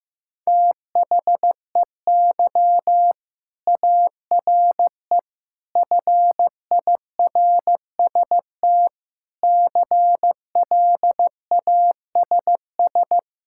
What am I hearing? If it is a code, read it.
Morse code, heard as a keyed tone, about 16 wpm, 700 Hz. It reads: THEY ARE FIRST CLASS
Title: Sentences from Top 1000 Words - Review Part 5 - 15wpm